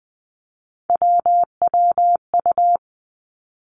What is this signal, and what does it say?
Morse code, heard as a keyed tone, about 20 wpm, 700 Hz. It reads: WWU
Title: Three Challenging Letter-Number Combinations 20wpm